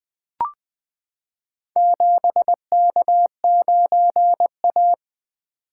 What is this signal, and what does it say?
Morse code, heard as a keyed tone, about 20 wpm, 700 Hz. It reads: 7K9A
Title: Four Letter-Number Combinations - 20wpm